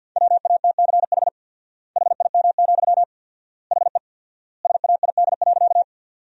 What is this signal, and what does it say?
Morse code, heard as a keyed tone, about 53 wpm, 700 Hz. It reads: WATCH HIM, HE SAID.
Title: 4x Word-Spacing - Sentences from Top 300 Words - 50wpm